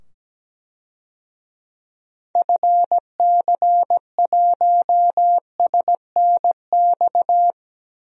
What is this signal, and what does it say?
Morse code, heard as a keyed tone, about 17 wpm, 700 Hz. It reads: FC1SNX
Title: DXCC-100 Call Signs - 6 Characters - 17wpm